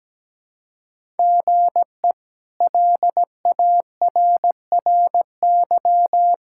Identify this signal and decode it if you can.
Morse code, heard as a keyed tone, about 17 wpm, 700 Hz. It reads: GE LARRY
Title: K1USN SST QSO Practice - 17wpm